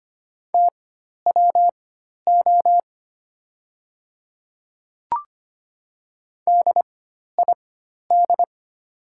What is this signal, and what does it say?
Morse code, heard as a keyed tone, about 25 wpm, spaced as for 12 wpm, 700 Hz. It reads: TWO DID
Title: 4x Character Spacing - Top 300 Words - 25wpm